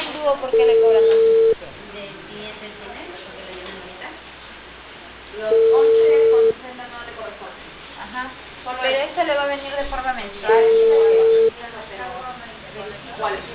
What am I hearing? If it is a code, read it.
Ringback tone (China, 450 Hz), with speech.